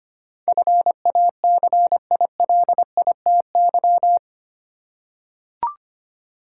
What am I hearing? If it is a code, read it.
Morse code, heard as a keyed tone, about 25 wpm, 690 Hz. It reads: FACILITY
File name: 8 Letter Words - First 3 Letters Given - 25wpm